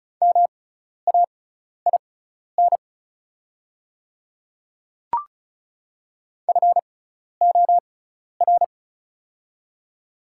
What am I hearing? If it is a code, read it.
Morse code, heard as a keyed tone, about 36 wpm, spaced as for 12 wpm, 690 Hz. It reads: MAIN FOR
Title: 6x Character Spacing - Top 300 Words - 35wpm